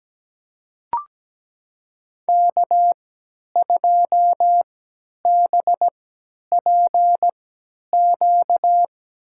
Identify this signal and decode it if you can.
Morse code, heard as a keyed tone, about 17 wpm, spaced as for 10 wpm, 700 Hz. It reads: K2BPQ
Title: USA Call Signs - 5 Characters - 3x Character Spacing - 17wpm